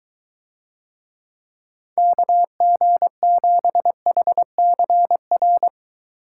Morse code, heard as a keyed tone, about 22 wpm, 700 Hz. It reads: KG7HCR